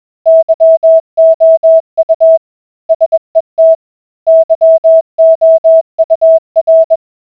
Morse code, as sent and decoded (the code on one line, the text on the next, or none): -.-- --- ..- / ... - / -.-- --- ..- .-.
YOU SET YOUR